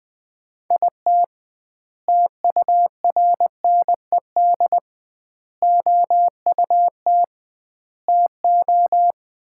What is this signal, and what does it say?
Morse code, heard as a keyed tone, about 20 wpm, 700 Hz. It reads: IT TURNED OUT TO
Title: 2x Word-Spacing - Sets of 5 Words - 20wpm